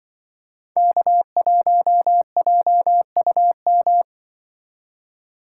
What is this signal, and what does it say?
Morse code, heard as a keyed tone, about 24 wpm, 700 Hz. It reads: K1JUM
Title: Speed Racing - USA 1x3 Call Signs 20wpm